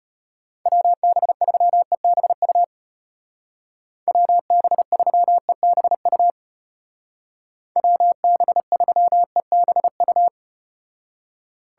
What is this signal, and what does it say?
Morse code, heard as a keyed tone, about 34 wpm, 700 Hz. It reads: WB3EBU WB3EBU WB3EBU